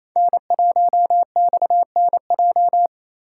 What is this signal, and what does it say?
Morse code, heard as a keyed tone, about 28 wpm, 700 Hz. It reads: N1XNJ
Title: USA Call Signs - 4x Character Spacing - 28wpm